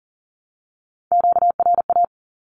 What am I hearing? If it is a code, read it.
Morse code, heard as a keyed tone, about 40 wpm, 700 Hz. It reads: QRA
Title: Q-Codes 40wpm